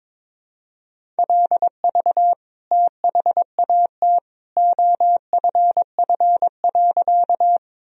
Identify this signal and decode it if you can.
Morse code, heard as a keyed tone, about 22 wpm, 700 Hz. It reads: LV THAT OFF.
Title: Sentences from Top 400 Words and Abbreviations 22wpm